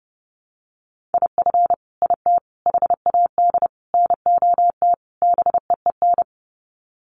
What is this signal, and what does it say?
Morse code, heard as a keyed tone, about 30 wpm, 700 Hz. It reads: IF IT HAD NOT BEEN
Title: Sets of 5 Words Encore 30wpm